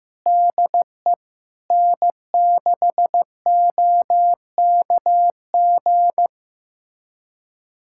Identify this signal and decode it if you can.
Morse code, heard as a keyed tone, about 15 wpm, 700 Hz. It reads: DE N6OKG
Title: USA Callsign DE Callsign - 5 Characters - 15wpm